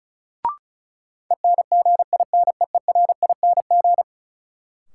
Morse code, heard as a keyed tone, about 34 wpm, 700 Hz. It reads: ENGINEERING
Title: Sentences from Top 1500 Words - New Word Drill - 35wpm